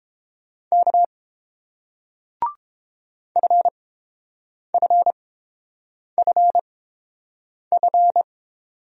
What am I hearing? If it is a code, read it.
Morse code, heard as a keyed tone, about 28 wpm, spaced as for 7 wpm, 690 Hz. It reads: K FFFF